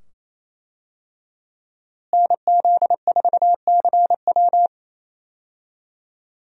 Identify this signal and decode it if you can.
Morse code, heard as a keyed tone, about 28 wpm, 700 Hz. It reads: NZ4CW
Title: USA Call Signs - 5 Characters - Repeated 2x - 28wpm